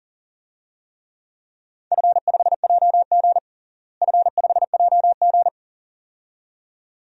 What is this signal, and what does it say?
Morse code, heard as a keyed tone, about 40 wpm, 700 Hz. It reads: F5JG F5JG
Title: DXCC-100 Call Signs - 4 Characters - Repeated 2x - 40wpm